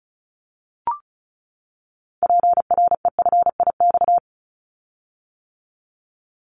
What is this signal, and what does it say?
Morse code, heard as a keyed tone, about 36 wpm, 700 Hz. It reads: PREFIX
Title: Callsign Prefixes for Top 100 DXCC - 35wpm